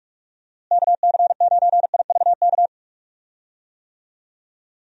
Morse code, heard as a keyed tone, about 41 wpm, 700 Hz. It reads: KC9IVK
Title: USA Call Signs - 6 Characters - Repeated 3x - 45wpm